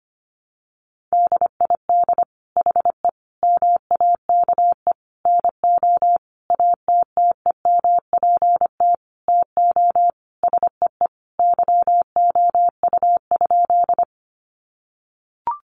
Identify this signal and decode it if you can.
Morse code, heard as a keyed tone, about 25 wpm, 700 Hz. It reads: DID HE MAKE NO ATTEMPT TO SEE YOU?
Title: Sentences from Top 1000 Words 25wpm